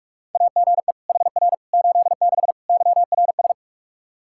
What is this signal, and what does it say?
Morse code, heard as a keyed tone, about 46 wpm, 700 Hz. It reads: AGE HR 86 YRS
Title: Practice QSO - Exchange 3 - More Info - 45wpm